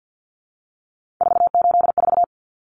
Morse code, heard as a keyed tone, about 48 wpm, 700 Hz. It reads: 484
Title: Common Cut Numbers - 3-Digit 50wpm